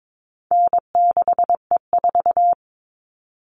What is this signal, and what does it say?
Morse code, heard as a keyed tone, about 22 wpm, 700 Hz. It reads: N6E4
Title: All Cut Numbers - 4-Digit 22wpm